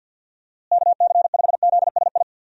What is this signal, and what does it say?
Morse code, heard as a keyed tone, about 51 wpm, 700 Hz. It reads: KC5ZSI
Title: USA Call Signs - 6 Characters - Repeated 2x - 50wpm